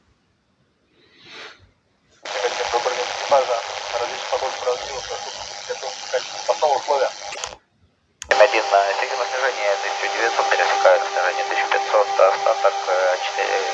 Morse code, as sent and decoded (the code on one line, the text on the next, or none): none